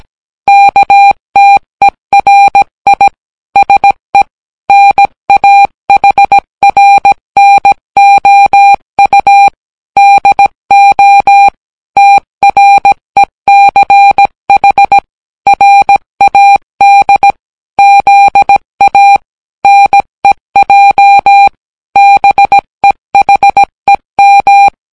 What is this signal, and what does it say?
Morse code, heard as a keyed tone, about 17 wpm, 800 Hz. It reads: KTERI SE NAHRNOU DO TRECH RAD ZA NEJ BEHEM